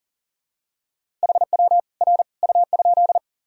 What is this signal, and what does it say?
Morse code, heard as a keyed tone, about 40 wpm, 700 Hz. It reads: HW R U?
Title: Sentences from Top 100 Words and Abbreviations 40wpm